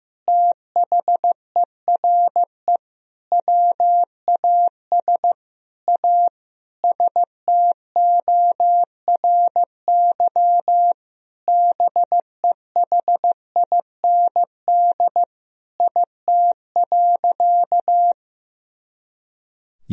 Morse code, heard as a keyed tone, about 15 wpm, 700 Hz. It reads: THERE WAS A STORY BEHIND IT.
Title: Sentences from Top 500 Words and Abbreviations 15wpm